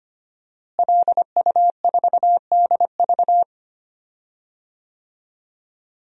Morse code, heard as a keyed tone, about 25 wpm, 700 Hz. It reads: LU4DV
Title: DXCC-100 Call Signs - 5 Characters - Repeated 2x - 25wpm